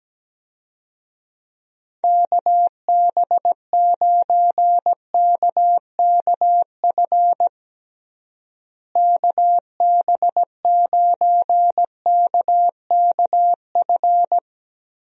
Morse code, heard as a keyed tone, about 17 wpm, 700 Hz. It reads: KB9KKF KB9KKF
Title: USA Call Signs - 6 Characters - Repeated 2x - 17wpm